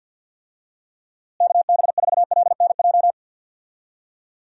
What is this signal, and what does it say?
Morse code, heard as a keyed tone, about 51 wpm, 700 Hz. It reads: KB4LNJ
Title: USA Call Signs - 6 Characters - Repeated 3x - 50wpm